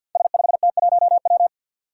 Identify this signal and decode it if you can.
Morse code, heard as a keyed tone, about 53 wpm, 700 Hz. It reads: S5T1W
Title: Five Challenging Letter-Number Combinations 50wpm